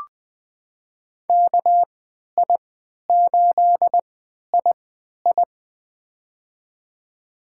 Morse code, heard as a keyed tone, about 20 wpm, spaced as for 11 wpm, 700 Hz. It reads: KI8II
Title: USA Call Signs - 3x Character Spacing - 20wpm